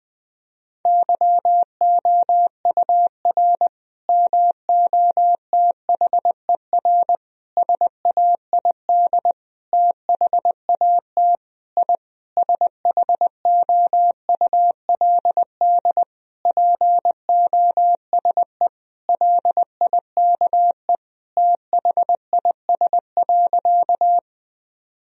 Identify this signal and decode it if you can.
Morse code, heard as a keyed tone, about 20 wpm, 700 Hz. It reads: YOUR MOTHER SAID THAT I SHOULD POSE LIKE THIS.
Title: Sentences from Top 1000 Words - Review Part 2 - 20wpm